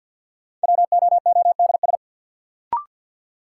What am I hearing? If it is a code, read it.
Morse code, heard as a keyed tone, about 47 wpm, 700 Hz. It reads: WOODS
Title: Sentences from Top 1000 Words - Review Part 2 - New Word Drill - 50wpm